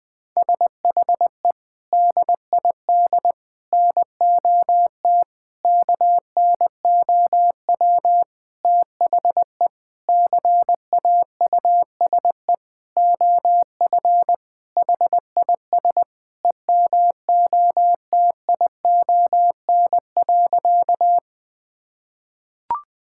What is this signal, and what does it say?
Morse code, heard as a keyed tone, about 20 wpm, 700 Hz. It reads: SHE DID NOT KNOW THE CAUSE OF HIS EMOTION.
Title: Sentences from Top 1600 Words 20wpm